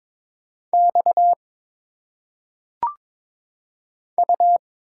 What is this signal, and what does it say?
Morse code, heard as a keyed tone, about 22 wpm, 700 Hz. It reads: X U
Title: Single Letters - Phonetic Pronunciation 22wpm